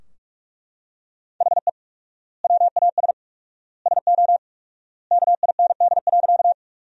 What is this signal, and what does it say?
Morse code, heard as a keyed tone, about 42 wpm, 700 Hz. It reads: HE WAS SO KIND.
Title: 4x Word-Spacing - Sentences from Top 200 Words - 45wpm